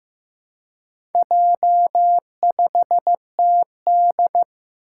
Morse code, heard as a keyed tone, about 15 wpm, 700 Hz. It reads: J5TD
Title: Four Challenging Letter-Number Combinations 15wpm